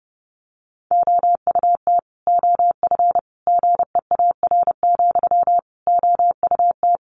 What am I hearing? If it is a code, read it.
Morse code, heard as a keyed tone, about 30 wpm, 700 Hz. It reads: OUT OF GEAR, OUT
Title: Sentences from Top 1400 Words 30wpm